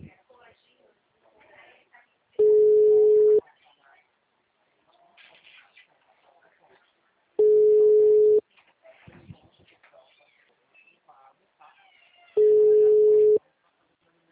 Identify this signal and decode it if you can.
Ringback tone (Europe, 425 Hz), with speech.